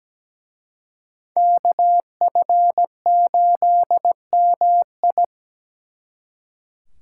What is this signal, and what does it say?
Morse code, heard as a keyed tone, about 20 wpm, 700 Hz. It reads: KF8MI
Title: Speed Racing - USA 2x2 Call Signs 17wpm